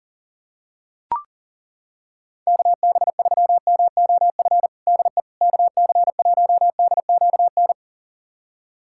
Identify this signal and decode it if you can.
Morse code, heard as a keyed tone, about 40 wpm, 700 Hz. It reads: KB3MOF DE KC1DQN
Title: USA Callsign DE Callsign 40wpm